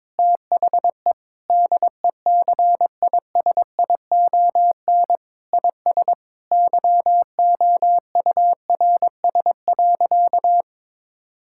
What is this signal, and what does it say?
Morse code, heard as a keyed tone, about 22 wpm, 700 Hz. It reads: THE DECISION IS YOURS.
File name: practice / 22wpm / Sentences with 8-Word Phrases - 22wpm